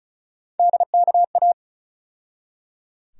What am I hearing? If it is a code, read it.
Morse code, heard as a keyed tone, about 34 wpm, 700 Hz. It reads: DKA
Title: USA Call Sign Postfixes - 3 Letters 35wpm